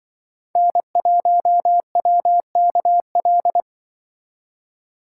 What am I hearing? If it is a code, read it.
Morse code, heard as a keyed tone, about 23 wpm, 700 Hz. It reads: N1WKL
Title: Speed Racing - USA 1x3 Call Signs 20wpm